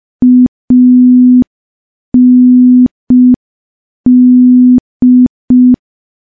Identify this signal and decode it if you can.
Morse code, heard as a keyed tone, about 5 wpm, 250 Hz. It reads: AND
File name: morse / morse-training-05WPM